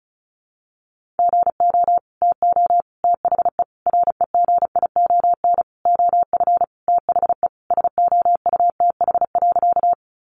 Morse code, heard as a keyed tone, about 34 wpm, 700 Hz. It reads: GO TO THE REGION OF THE SOUTH.
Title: Sentences from Top 1500 Words 35wpm